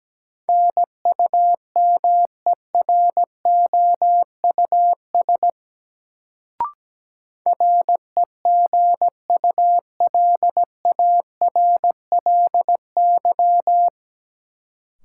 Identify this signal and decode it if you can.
Morse code, heard as a keyed tone, about 17 wpm, 700 Hz. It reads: NUMEROUS REGULARLY
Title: Sentences from Top 1400 Words - New Word Drill Alt - 17wpm